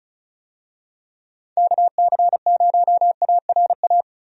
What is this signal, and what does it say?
Morse code, heard as a keyed tone, about 34 wpm, 700 Hz. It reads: KC0ARA
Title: USA Call Signs - 6 Characters - Repeated 2x - 35wpm